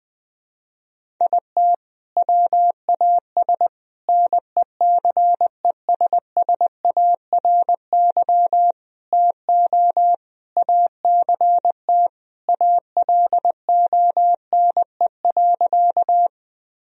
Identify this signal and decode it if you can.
Morse code, heard as a keyed tone, about 20 wpm, 700 Hz. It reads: IT WAS NECESSARY TO ACT ALONE.
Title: Sentences from Top 1000 Words - Review Part 1 - 20wpm